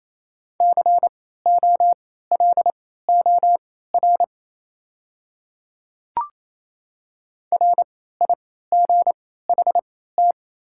Morse code, heard as a keyed tone, about 28 wpm, spaced as for 16 wpm, 700 Hz. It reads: COLOR RIGHT
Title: 3x Character Spacing - Top 300 Words - 28wpm